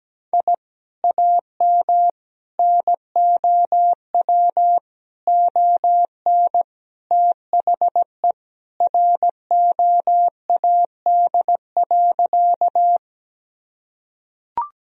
Morse code, heard as a keyed tone, about 17 wpm, 700 Hz. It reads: I AM NOW ON THE ROAD.
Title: Sentences from Top 1000 Words - Review Part 2 - 17wpm